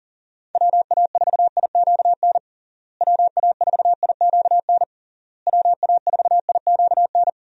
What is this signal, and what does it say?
Morse code, heard as a keyed tone, about 37 wpm, 700 Hz. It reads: WA4IQN WA4IQN WA4IQN
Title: USA Call Signs - 6 Characters - Repeated 3x - 40wpm